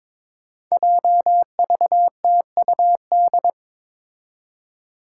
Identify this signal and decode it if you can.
Morse code, heard as a keyed tone, about 22 wpm, 700 Hz. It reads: JVTUD
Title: Five Challenging Letter-Number Combinations 22wpm